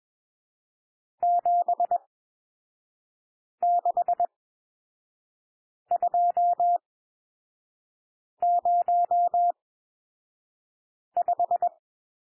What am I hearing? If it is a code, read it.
Morse code, heard as a keyed tone, about 21 wpm, spaced as for 5 wpm, 700 Hz. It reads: 76205
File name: cwmp3 / N1 SOUND NUMBERS 21 wpm